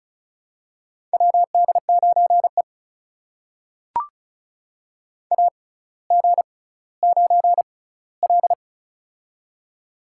Morse code, heard as a keyed tone, about 34 wpm, spaced as for 18 wpm, 700 Hz. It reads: WD9E AG9L
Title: USA Call Signs - 4 Characters - 6x Character Spacing - 35wpm